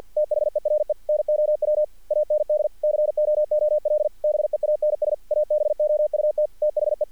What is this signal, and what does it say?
Morse code, heard as a keyed tone, about 48 wpm, 600 Hz. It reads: THERE NOW AND COOL BEANS ABOUT THE
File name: archived / CRSnet